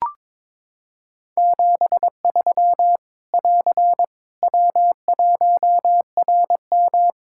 Morse code, heard as a keyed tone, about 22 wpm, 700 Hz. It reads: 73 <AR> W1RM
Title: Practice QSO - Exchange 4 - Ending - 22wpm